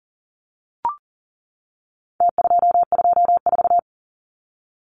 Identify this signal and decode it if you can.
Morse code, heard as a keyed tone, about 40 wpm, 700 Hz. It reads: T224
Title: Numbers - Contest Style - 4-Digit 40wpm